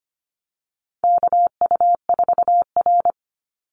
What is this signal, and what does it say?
Morse code, heard as a keyed tone, about 25 wpm, 700 Hz. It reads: KU4R
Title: NA SOTA Chaser Callsigns 25wpm